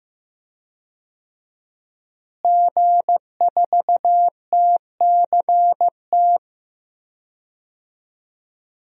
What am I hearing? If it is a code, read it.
Morse code, heard as a keyed tone, about 15 wpm, 700 Hz. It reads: G4TCT
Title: DXCC-100 Call Signs - 5 Characters - Repeated 2x - 15wpm